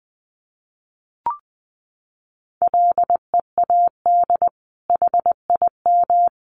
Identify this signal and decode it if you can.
Morse code, heard as a keyed tone, about 19 wpm, 700 Hz. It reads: LEAD HIM
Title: Sentences from Top 1200 Words 20wpm